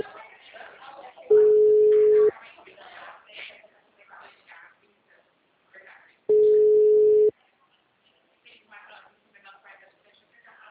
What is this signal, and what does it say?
Ringback tone (Europe, 425 Hz), with speech.